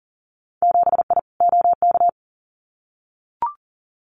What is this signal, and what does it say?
Morse code, heard as a keyed tone, about 39 wpm, 700 Hz. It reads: 7I OK
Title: Winter Field Day Exchanges Encore 40wpm